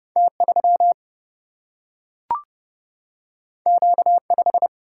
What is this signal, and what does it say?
Morse code, heard as a keyed tone, about 30 wpm, 700 Hz. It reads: T 3 Q 5
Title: Single-Letter Single-Number 30wpm